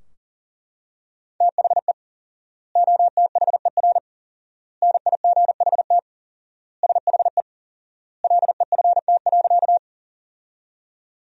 Morse code, heard as a keyed tone, about 40 wpm, 700 Hz. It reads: THE OTHER NIGHT SHE LEFT.
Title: 4x Word-Spacing - Sentences from Top 300 Words - 40wpm